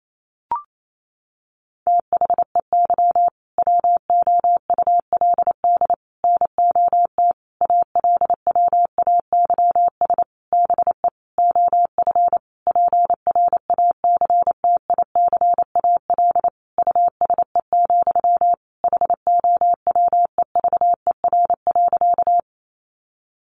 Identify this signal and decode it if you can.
Morse code, heard as a keyed tone, about 28 wpm, 700 Hz. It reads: THEY WOULD NOT ALWAYS BE OF PRACTICAL USE, HOWEVER.
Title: Sentences from Top 1400 Words 28wpm